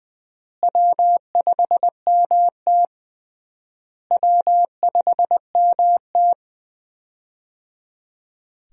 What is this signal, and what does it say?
Morse code, heard as a keyed tone, about 21 wpm, 700 Hz. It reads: W5MT W5MT
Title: USA Call Signs - 4 Characters - Repeated 3x - 20wpm